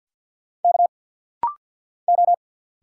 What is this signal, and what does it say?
Morse code, heard as a keyed tone, about 50 wpm, 700 Hz. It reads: K O
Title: Single Letter-Number - Rapid-Fire 50wpm